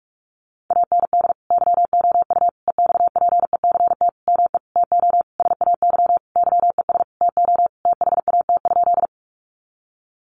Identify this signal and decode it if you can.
Morse code, heard as a keyed tone, about 44 wpm, 700 Hz. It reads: AND YOU EXPECT ME TO SAY YES TO THAT?
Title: Sentences from Top 1100 Words 45wpm